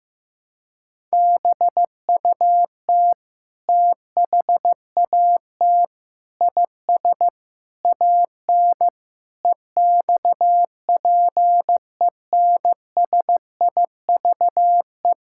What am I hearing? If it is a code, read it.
Morse code, heard as a keyed tone, about 15 wpm, 700 Hz. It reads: BUT THAT IS AN EXPENSIVE
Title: Sentences from Top 1000 Words - Review Part 6 - 15wpm